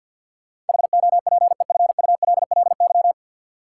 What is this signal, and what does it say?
Morse code, heard as a keyed tone, about 53 wpm, 700 Hz. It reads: HOPEFULLY
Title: Sentences from Top 1500 Words - New Word Drill - 50wpm